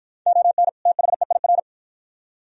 Morse code, heard as a keyed tone, about 48 wpm, 700 Hz. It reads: ON THEIR